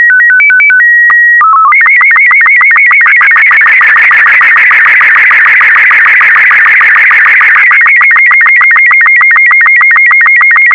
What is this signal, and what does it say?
SSTV (slow-scan television), Robot 36.